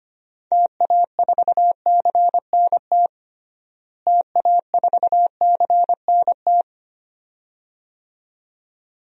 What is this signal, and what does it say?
Morse code, heard as a keyed tone, about 25 wpm, 700 Hz. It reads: TA4CNT TA4CNT
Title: DXCC-100 Call Signs - 6 Characters - Repeated 2x - 25wpm